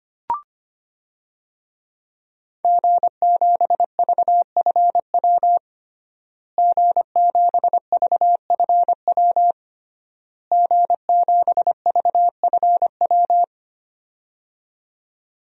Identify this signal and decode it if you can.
Morse code, heard as a keyed tone, about 25 wpm, 700 Hz. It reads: G7VFW G7VFW G7VFW
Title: DXCC-100 Call Signs - 5 Characters - Repeated 3x - 25wpm